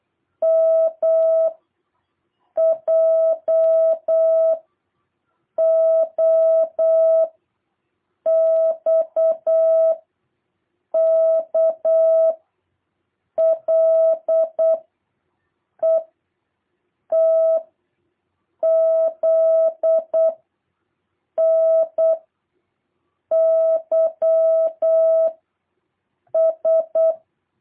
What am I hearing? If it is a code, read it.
Morse code, heard as a keyed tone, about 8 wpm, 650 Hz. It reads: M J O X K L E T Z N Y S
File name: jh8cbh / 2023cw